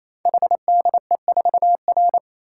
Morse code, heard as a keyed tone, about 29 wpm, 700 Hz. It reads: HDE4R
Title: Code Groups - 5 Letter:Numbers - 28wpm